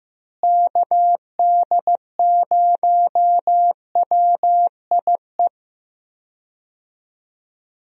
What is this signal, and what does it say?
Morse code, heard as a keyed tone, about 15 wpm, 700 Hz. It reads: KD0WIE
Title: USA Call Signs - 6 Characters - Repeated 2x - 15wpm